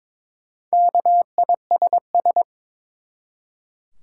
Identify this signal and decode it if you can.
Morse code, heard as a keyed tone, about 23 wpm, 700 Hz. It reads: KISS
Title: Four Letter Words - 22wpm